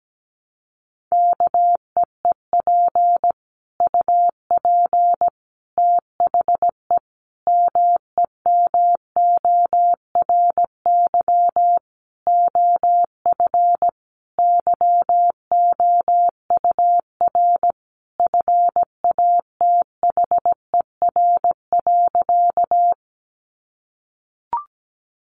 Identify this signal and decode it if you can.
Morse code, heard as a keyed tone, about 17 wpm, 700 Hz. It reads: KEEP UP THE MEMORY OF YOUR FATHER.
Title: Sentences from Top 1300 Words 17wpm